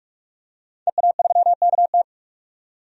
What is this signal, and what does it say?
Morse code, heard as a keyed tone, about 45 wpm, 700 Hz. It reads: EA3KT
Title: DXCC-100 Call Signs - 5 Characters - 45wpm